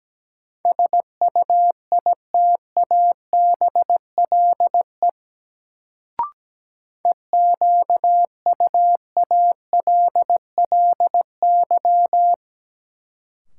Morse code, heard as a keyed tone, about 17 wpm, 700 Hz. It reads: SUITABLE EQUALLY
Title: Sentences from Top 1400 Words - New Word Drill Alt - 17wpm